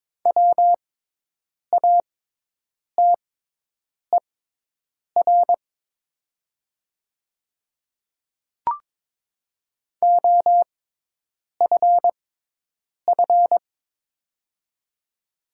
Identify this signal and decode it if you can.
Morse code, heard as a keyed tone, about 22 wpm, spaced as for 8 wpm, 700 Hz. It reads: WATER OFF
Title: 6x Character Spacing - Top 300 Words - 22wpm